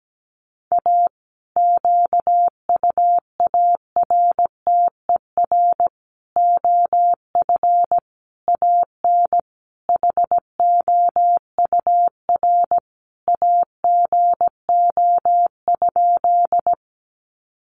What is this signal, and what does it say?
Morse code, heard as a keyed tone, about 17 wpm, 700 Hz. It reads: A QUARTER OF AN HOUR AGO?
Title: Sentences from Top 1300 Words 17wpm